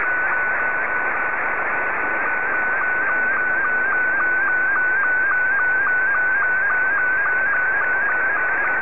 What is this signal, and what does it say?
SSTV (slow-scan television), Scottie 2.
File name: sstv / S2-VK6FRLR